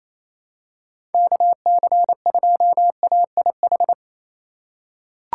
Morse code, heard as a keyed tone, about 28 wpm, 700 Hz. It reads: KC2AIH
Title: USA Call Signs - 6 Characters - 3x Character Spacing - 28wpm